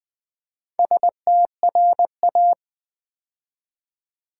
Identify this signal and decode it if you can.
Morse code, heard as a keyed tone, about 20 wpm, 700 Hz. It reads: STRA
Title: Build A Word - 2 Letters At A Time - 8-Letter Words - 20wpm